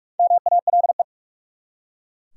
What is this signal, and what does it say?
Morse code, heard as a keyed tone, about 44 wpm, 700 Hz. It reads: MALE
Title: Four Letter Words - 45wpm